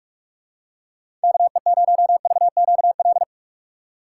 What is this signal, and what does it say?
Morse code, heard as a keyed tone, about 45 wpm, 700 Hz. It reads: KE0VQL
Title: USA Call Signs - 6 Characters - 6x Character Spacing - 45wpm